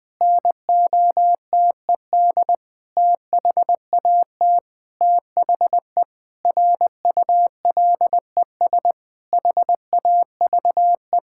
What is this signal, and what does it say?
Morse code, heard as a keyed tone, about 20 wpm, 700 Hz. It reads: NOTED THAT THE RULES HAVE
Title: Sentences with 5-Word Phrases - Spoken Beginning - 20wpm